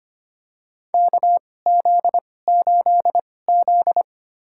Morse code, heard as a keyed tone, about 25 wpm, spaced as for 18 wpm, 700 Hz. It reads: KZ8Z
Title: USA Call Signs - 4 Characters - 2x Character Spacing - 25wpm